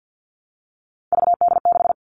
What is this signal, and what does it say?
Morse code, heard as a keyed tone, about 50 wpm, 700 Hz. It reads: VD6
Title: Three Challenging Letter-Number Combinations 50wpm